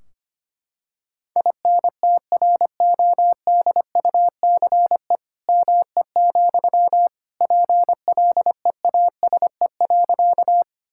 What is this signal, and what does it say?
Morse code, heard as a keyed tone, about 25 wpm, 700 Hz. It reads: INTRODUCE ME, PLEASE.
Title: Sentences from Top 1600 Words 25wpm